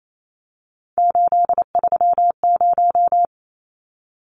Morse code, heard as a keyed tone, about 28 wpm, 700 Hz. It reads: 830
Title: Numbers - 3-Digit 28wpm